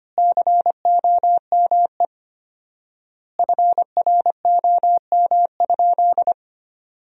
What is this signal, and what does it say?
Morse code, heard as a keyed tone, about 25 wpm, 700 Hz. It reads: COME FROM?
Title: 4x Word-Spacing - Sentences from Top 300 Words - 25wpm